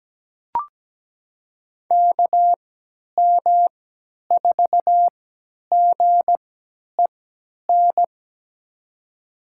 Morse code, heard as a keyed tone, about 17 wpm, spaced as for 10 wpm, 700 Hz. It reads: KM4GEN
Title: USA Call Signs - 3x Character Spacing - 17wpm